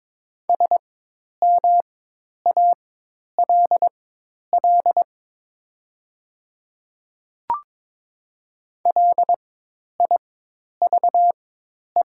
Morse code, heard as a keyed tone, about 22 wpm, spaced as for 10 wpm, 700 Hz. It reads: SMALL LIVE
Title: 4x Character Spacing - Top 200 Words - 22wpm